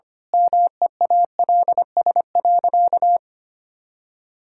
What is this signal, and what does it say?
Morse code, heard as a keyed tone, about 25 wpm, 700 Hz. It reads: MEALS.
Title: Pangram Challenge v2 - 25wpm